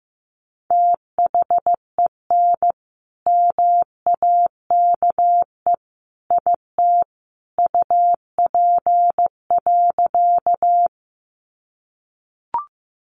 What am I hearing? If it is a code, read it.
Morse code, heard as a keyed tone, about 15 wpm, 700 Hz. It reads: THEN MAKE IT UP.
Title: Sentences from Top 100 Words 15wpm